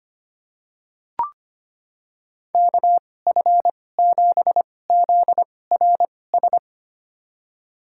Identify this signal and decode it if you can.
Morse code, heard as a keyed tone, about 25 wpm, spaced as for 18 wpm, 700 Hz. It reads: KF7ZRS
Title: USA Call Signs - 6 Characters - 2x Character Spacing - 25wpm